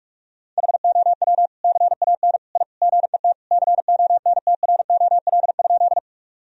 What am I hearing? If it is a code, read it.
Morse code, heard as a keyed tone, about 41 wpm, 700 Hz. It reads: HOW CAN I GET CONTROL?
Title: Sentences from Top 1000 Words - Review Part 3 - 45wpm